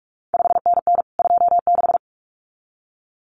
Morse code, heard as a keyed tone, about 46 wpm, 700 Hz. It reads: 5NN 26
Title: CQ WW DX Contest Exchange Encore - 45wpm